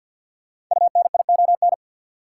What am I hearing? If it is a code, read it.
Morse code, heard as a keyed tone, about 53 wpm, 700 Hz. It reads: UNION
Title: Sentences from Top 1400 Words - New Word Drill - 50wpm